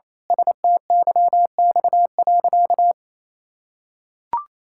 Morse code, heard as a keyed tone, about 28 wpm, 700 Hz. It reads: STYX.
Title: Pangram Challenge v2 - 28wpm